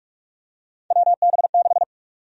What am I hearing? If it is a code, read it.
Morse code, heard as a keyed tone, about 42 wpm, 700 Hz. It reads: WB6
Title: USA Call Sign Prefixes - Repeated 2x - 45wpm